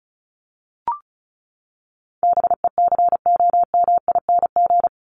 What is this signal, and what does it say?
Morse code, heard as a keyed tone, about 34 wpm, 700 Hz. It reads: BECOMING
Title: ING Suffix - Top 100 Words with Base Words 35wpm